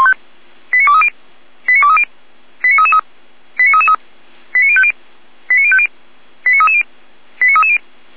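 Five-tone selective call (ZVEI), heard as quick runs of five tones, tones of about 70 ms. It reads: 89230 89230 89333 89333 89050 89050 89330 89330
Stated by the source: ZVEI five-tone call 89304 received by SDR.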